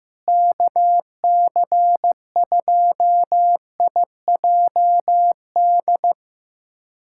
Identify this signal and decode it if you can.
Morse code, heard as a keyed tone, about 15 wpm, 700 Hz. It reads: KC2IJD
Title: USA Callsign DE Callsign - 6 Characters - 15wpm